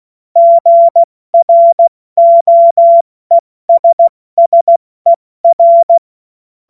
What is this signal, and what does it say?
Morse code, heard as a keyed tone, about 16 wpm, 680 Hz. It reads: GROESSER
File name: Morsekurs / Witze / Witz 12. April 2022 70 F80